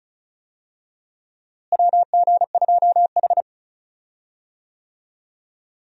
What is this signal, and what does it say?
Morse code, heard as a keyed tone, about 36 wpm, 700 Hz. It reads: WG2H
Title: USA Call Signs - 4 Characters - Repeated 2x - 35wpm